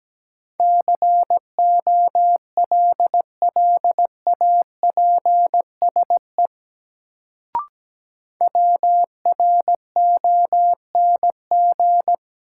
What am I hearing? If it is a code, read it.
Morse code, heard as a keyed tone, about 17 wpm, 700 Hz. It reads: COLLAPSE WRONG
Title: Sets of 5 Words Encore - New Word Drill - 17wpm